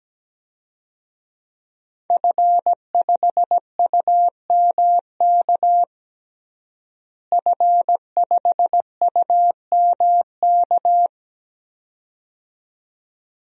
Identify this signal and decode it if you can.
Morse code, heard as a keyed tone, about 17 wpm, 700 Hz. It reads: F5UMK F5UMK
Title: DXCC-100 Call Signs - 5 Characters - Repeated 2x - 17wpm